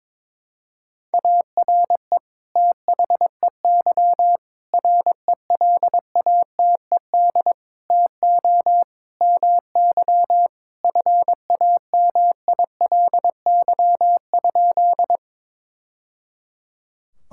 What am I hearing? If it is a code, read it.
Morse code, heard as a keyed tone, about 21 wpm, 700 Hz. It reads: ARE THEY RELATED TO MY FAMILY?
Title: Sentences from Top 1000 Words - Review Part 5 - 22wpm